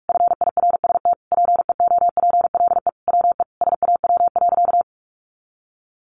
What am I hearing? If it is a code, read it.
Morse code, heard as a keyed tone, about 44 wpm, 700 Hz. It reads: FIRST PEOPLE WE SAW.
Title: Sentences from Top 300 Words 45wpm